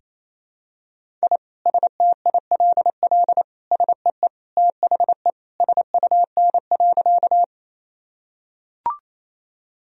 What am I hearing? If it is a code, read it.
Morse code, heard as a keyed tone, about 28 wpm, 700 Hz. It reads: I STILL SEE THE SUN.
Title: Sentences from Top 1000 Words - Review Part 1 - 28wpm